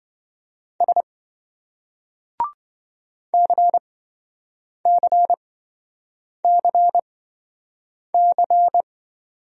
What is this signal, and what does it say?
Morse code, heard as a keyed tone, about 26 wpm, spaced as for 7 wpm, 700 Hz. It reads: S CCCC